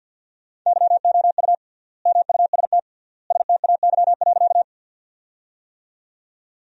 Morse code, heard as a keyed tone, about 50 wpm, 700 Hz. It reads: YOU MUST STAY.
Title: 3x Word-Spacing - Sentences from Top 500 Words - 50wpm